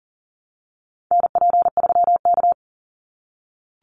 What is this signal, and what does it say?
Morse code, heard as a keyed tone, about 38 wpm, 700 Hz. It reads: NP3K
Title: Call Signs - 40wpm